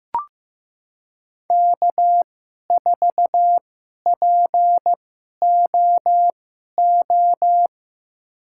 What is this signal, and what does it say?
Morse code, heard as a keyed tone, about 15 wpm, spaced as for 11 wpm, 700 Hz. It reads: K4POO
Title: USA Call Signs - 5 Characters - 2x Character Spacing - 15wpm